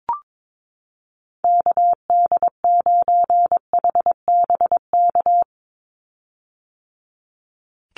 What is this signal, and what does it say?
Morse code, heard as a keyed tone, about 22 wpm, 700 Hz. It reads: KD9HBK
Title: USA Call Signs 22wpm